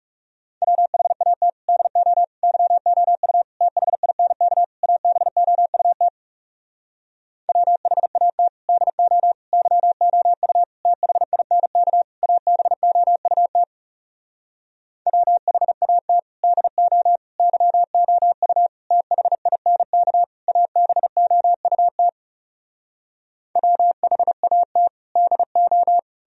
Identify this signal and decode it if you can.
Morse code, heard as a keyed tone, about 37 wpm, 700 Hz. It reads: WHAT DO YOU THINK ABOUT WHAT DO YOU THINK ABOUT WHAT DO YOU THINK ABOUT WHAT DO
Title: Speed-Racing - Sets of 5 Words 30wpm